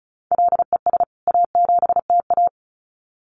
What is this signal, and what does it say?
Morse code, heard as a keyed tone, about 34 wpm, 700 Hz. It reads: LES A7TA
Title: CWT Exchange 35wpm